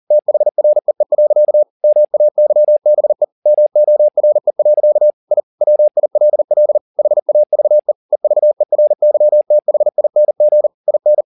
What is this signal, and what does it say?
Morse code, heard as a keyed tone, about 40 wpm, 600 Hz. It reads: THREE. MAYBE MORE. I WILL HAVE EVERYTHING IN
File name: the old man and the sea 2 40wpm